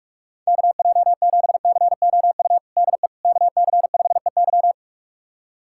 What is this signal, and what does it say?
Morse code, heard as a keyed tone, about 46 wpm, 700 Hz. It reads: KJ7COU DE KC5EY
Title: USA Callsign DE Callsign 45wpm